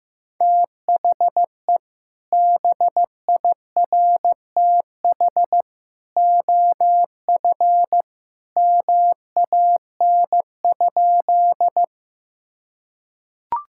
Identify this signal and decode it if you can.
Morse code, heard as a keyed tone, about 15 wpm, 700 Hz. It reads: THE BIRTH OF MAN?
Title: Sentences from Top 1000 Words - Review Part 5 - 15wpm